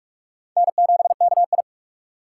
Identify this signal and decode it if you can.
Morse code, heard as a keyed tone, about 46 wpm, 700 Hz. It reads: N7KI
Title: USA Call Signs - 4 Characters - 3x Character Spacing - 45wpm